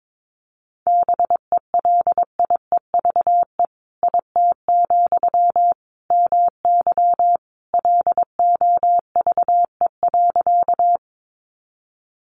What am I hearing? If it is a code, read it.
Morse code, heard as a keyed tone, about 22 wpm, 700 Hz. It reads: BELIEVE IT, MY LOVE.
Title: Sentences from Top 1000 Words 22wpm